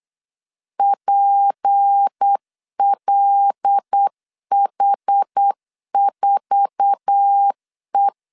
Morse code, heard as a keyed tone, about 8 wpm, 800 Hz. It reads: PLH4E
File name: LEZIONI / 8 YJ45 / 8 YJ4 8wpm 3246